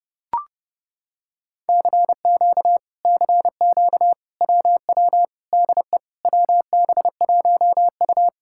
Morse code, heard as a keyed tone, about 29 wpm, 700 Hz. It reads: CQ CQ WW DE WB1U